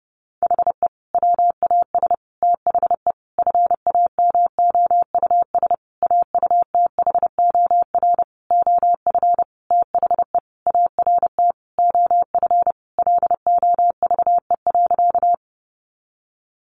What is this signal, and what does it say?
Morse code, heard as a keyed tone, about 30 wpm, 700 Hz. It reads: HE WAS THE FAMOUS AUTHOR OF THE ART OF LOVE.
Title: Sentences from Top 1500 Words 30wpm